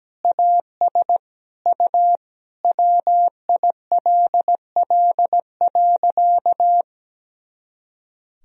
Morse code, heard as a keyed tone, about 17 wpm, 700 Hz. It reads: AS U WILL.
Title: Sentences from Top 100 Words and Abbreviations 17wpm